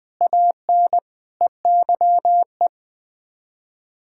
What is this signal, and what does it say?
Morse code, heard as a keyed tone, about 21 wpm, 700 Hz. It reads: AN EYE
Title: Sets of 6 Words - 20wpm